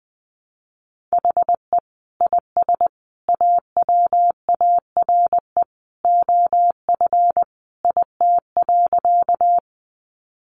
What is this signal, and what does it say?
Morse code, heard as a keyed tone, about 20 wpm, 700 Hz. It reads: HE IS AWARE OF IT.
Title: Sentences from Top 1200 Words 20wpm